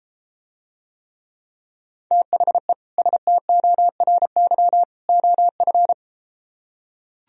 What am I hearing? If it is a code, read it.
Morse code, heard as a keyed tone, about 34 wpm, 700 Hz. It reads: THE STORY OF